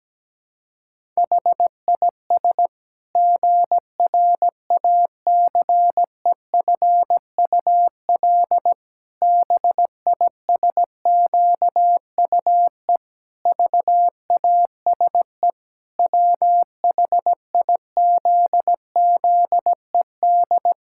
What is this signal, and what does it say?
Morse code, heard as a keyed tone, about 17 wpm, 700 Hz. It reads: HIS GRACEFUL BISQUE VASE WHIZZED
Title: Pangram Challenge v2 - 17wpm